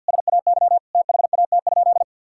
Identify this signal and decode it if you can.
Morse code, heard as a keyed tone, about 50 wpm, 700 Hz. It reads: SAY THAT?
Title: Sentences from Top 1000 Words - Review Part 5 - 50wpm